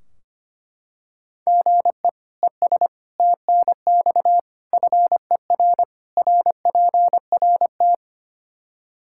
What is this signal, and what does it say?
Morse code, heard as a keyed tone, about 25 wpm, 700 Hz. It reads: GE ES TNX FER RPRT
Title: Practice QSO - Exchange 1 - Beginning - 25wpm